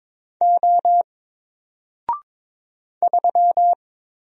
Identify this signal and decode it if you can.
Morse code, heard as a keyed tone, about 22 wpm, 700 Hz. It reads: O 3
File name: Single-Letter-Number-22wpm